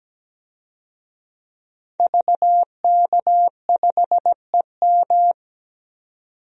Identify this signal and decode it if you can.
Morse code, heard as a keyed tone, about 17 wpm, 700 Hz. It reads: VK5EM
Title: DXCC-100 Call Signs - 5 Characters - 17wpm